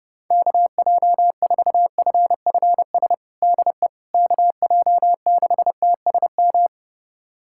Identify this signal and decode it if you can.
Morse code, heard as a keyed tone, about 30 wpm, 700 Hz. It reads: KJ4FFS DE KJ6TSM
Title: USA Callsign DE Callsign - 6 Characters - 30wpm